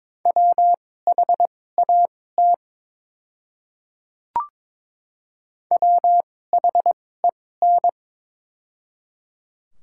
Morse code, heard as a keyed tone, about 22 wpm, spaced as for 16 wpm, 700 Hz. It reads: WHAT WHEN
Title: 2x Character Spacing - Top 100 Words - 22wpm